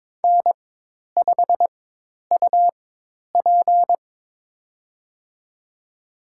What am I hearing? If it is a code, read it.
Morse code, heard as a keyed tone, about 23 wpm, spaced as for 10 wpm, 700 Hz. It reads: N5UP
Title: USA Call Signs - 4 Characters - 4x Character Spacing - 22wpm